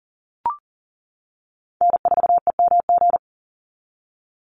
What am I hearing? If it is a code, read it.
Morse code, heard as a keyed tone, about 40 wpm, 700 Hz. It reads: N4EMG
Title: NA SOTA Chaser Callsigns 40wpm